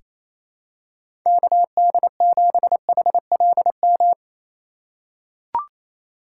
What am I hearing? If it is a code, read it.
Morse code, heard as a keyed tone, about 28 wpm, 700 Hz. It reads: KD7HLM
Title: USA Call Signs - 6 Characters - 4x Character Spacing - 28wpm